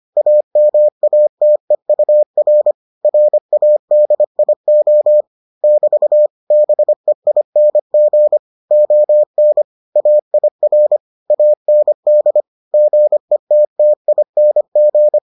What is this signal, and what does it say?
Morse code, heard as a keyed tone, about 25 wpm, 600 Hz. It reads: AMATEUR RADIO <BT> BEING ON AIR AND GETTING